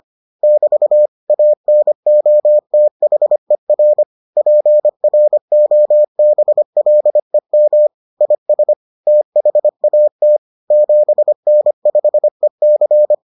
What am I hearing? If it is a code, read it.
Morse code, heard as a keyed tone, about 25 wpm, 600 Hz. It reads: <BT> ANOTHER PROBLEM IS THAT 7N5EC